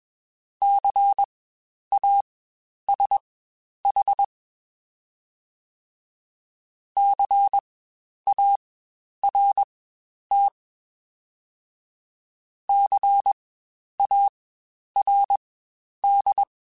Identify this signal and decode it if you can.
Morse code, heard as a keyed tone, about 21 wpm, spaced as for 10 wpm, 800 Hz. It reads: CASH CART CARD
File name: Spaced Alphas / Spaced Alphas mp3s / Cs sp @20